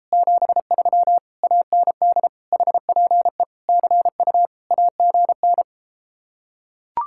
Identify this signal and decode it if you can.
Morse code, heard as a keyed tone, about 34 wpm, 700 Hz. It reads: 73 AND HPE CU AGN